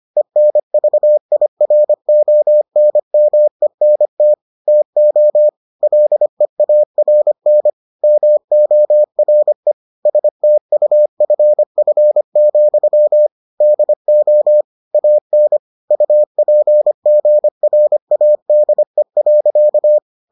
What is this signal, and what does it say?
Morse code, heard as a keyed tone, about 25 wpm, 600 Hz. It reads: ENVIRONMENT TO LEARN MORE STUFF, DO AN UPGRADE.